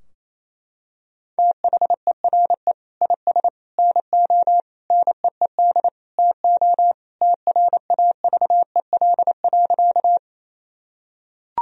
Morse code, heard as a keyed tone, about 27 wpm, 700 Hz. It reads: THERE IS NO NEED TO TRAVEL.
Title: Sentences from Top 1000 Words - Review Part 2 - 28wpm